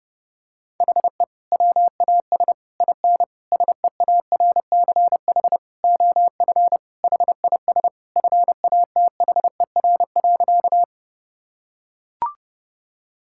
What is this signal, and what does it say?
Morse code, heard as a keyed tone, about 32 wpm, 700 Hz. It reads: HE WAS IN SEARCH OF HIS FATHER.
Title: Sentences from Top 1000 Words - Review Part 5 - 30wpm